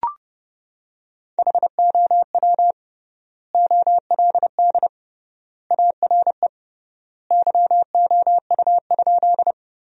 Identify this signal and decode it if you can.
Morse code, heard as a keyed tone, about 31 wpm, 700 Hz. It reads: HOW OLD ARE YOU?
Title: 3x Word-Spacing - Sentences from Top 200 Words - 30wpm